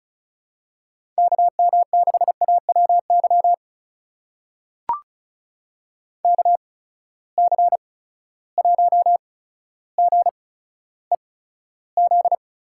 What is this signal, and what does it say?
Morse code, heard as a keyed tone, about 36 wpm, spaced as for 15 wpm, 700 Hz. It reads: KM6AWY KC1GEZ
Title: USA Call Signs - 6 Characters - 8x Character Spacing - 35wpm